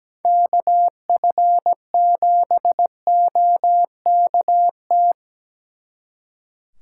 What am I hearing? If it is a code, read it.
Morse code, heard as a keyed tone, about 18 wpm, 700 Hz. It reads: KF7OKT
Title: Speed Racing - USA Call Signs 17wpm